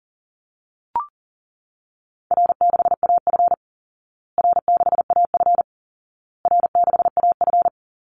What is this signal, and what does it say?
Morse code, heard as a keyed tone, about 40 wpm, 700 Hz. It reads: R6AF R6AF R6AF
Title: Easy Call Signs Repeated 3x - 40wpm